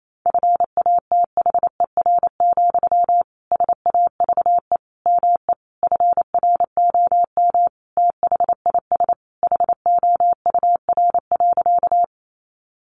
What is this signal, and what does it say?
Morse code, heard as a keyed tone, about 28 wpm, 700 Hz. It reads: FATHER, SAVE ME FROM THIS HOUR.
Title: Sentences from Top 1100 Words 28wpm